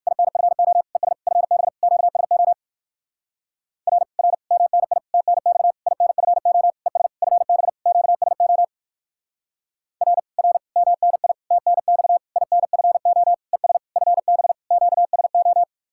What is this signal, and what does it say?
Morse code, heard as a keyed tone, about 56 wpm, 690 Hz. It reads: INFO ES FB QSO R R MNI TNX INFO ES FB QSO R R MNI TNX INFO ES FB QSO
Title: Speed-Racing Common QSO Phrases - 45wpm